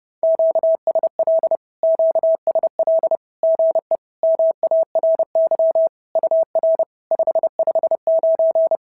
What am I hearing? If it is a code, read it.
Morse code, heard as a keyed tone, about 30 wpm, 650 Hz. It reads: QSL QSL GE MARY UR 559
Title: SOTA QSO Practice 30wpm